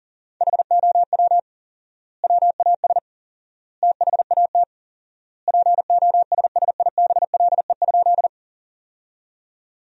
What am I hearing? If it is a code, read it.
Morse code, heard as a keyed tone, about 40 wpm, 700 Hz. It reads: HOW WAS THAT POSSIBLE?
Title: 4x Word-Spacing - Sentences from Top 500 Words - 40wpm